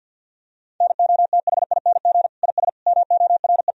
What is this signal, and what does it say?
Morse code, heard as a keyed tone, about 50 wpm, 700 Hz. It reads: NOTHING IS MORE
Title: Sentences from Top 1000 Words - Review Part 4 - 50wpm